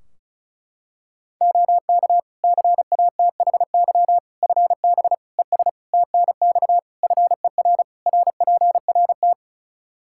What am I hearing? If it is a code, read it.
Morse code, heard as a keyed tone, about 34 wpm, 700 Hz. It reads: OK CATHY FB ES TNX FER RPRT
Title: K1USN SST QSO Practice 35wpm